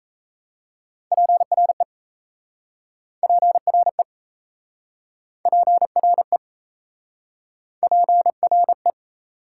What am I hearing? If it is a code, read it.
Morse code, heard as a keyed tone, about 38 wpm, 700 Hz. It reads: PRE PRE PRE PRE